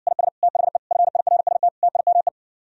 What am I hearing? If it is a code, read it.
Morse code, heard as a keyed tone, about 59 wpm, 700 Hz. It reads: IS THE FIRST TIME